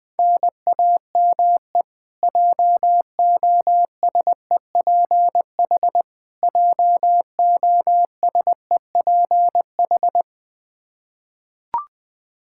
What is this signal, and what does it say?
Morse code, heard as a keyed tone, about 20 wpm, 700 Hz. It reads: NAME JOSEPH JOSEPH
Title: Practice QSO - Exchange 1 - Beginning - 20wpm